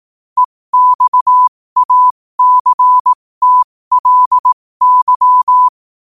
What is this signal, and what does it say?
Morse code, heard as a keyed tone, about 18 wpm, spaced as for 15 wpm, 1000 Hz. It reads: EXACTLY
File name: qotdfiles / 15wpm